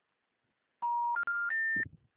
Special information tone.